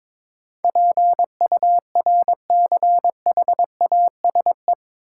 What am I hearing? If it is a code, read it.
Morse code, heard as a keyed tone, about 21 wpm, 700 Hz. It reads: PURCHASE
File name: practice / 22wpm / Eight Letter Words - 22wpm